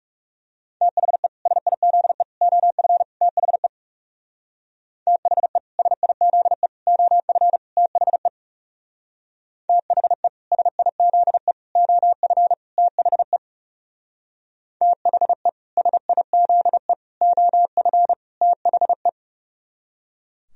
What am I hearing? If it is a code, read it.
Morse code, heard as a keyed tone, about 37 wpm, 700 Hz. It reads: THE SIZE OF THE THE SIZE OF THE THE SIZE OF THE THE SIZE OF THE
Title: Speed-Racing - Sets of 4 Words 30wpm